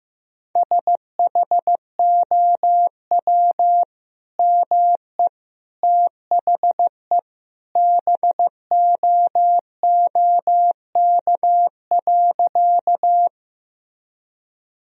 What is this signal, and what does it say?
Morse code, heard as a keyed tone, about 15 wpm, 700 Hz. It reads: SHOW ME THE BOOK.
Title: Sentences from Top 1000 Words - Review Part 1 - 15wpm